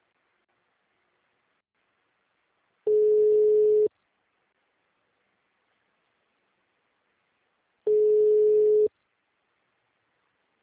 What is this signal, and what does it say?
Ringback tone (Europe, 425 Hz).
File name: out- -2005- - - .60